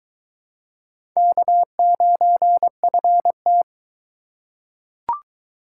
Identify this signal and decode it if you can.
Morse code, heard as a keyed tone, about 22 wpm, 700 Hz. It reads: K9FT